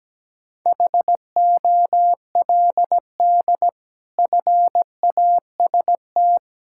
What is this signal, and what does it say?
Morse code, heard as a keyed tone, about 17 wpm, 700 Hz. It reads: HOLD FAST
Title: Sentences from Top 1600 Words 17wpm